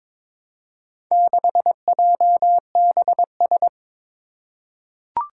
Morse code, heard as a keyed tone, about 22 wpm, 700 Hz. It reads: 6JBS
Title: Four Challenging Letter-Number Combinations 22wpm